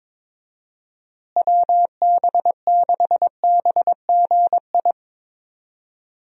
Morse code, heard as a keyed tone, about 21 wpm, 700 Hz. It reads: WB6BGI
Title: USA Call Signs - 6 Characters - Repeated 2x - 22wpm